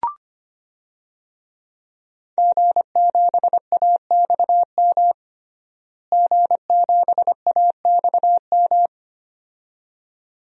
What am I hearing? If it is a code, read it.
Morse code, heard as a keyed tone, about 25 wpm, 700 Hz. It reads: G7AXM G7AXM
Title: DXCC-100 Call Signs - 5 Characters - Repeated 2x - 25wpm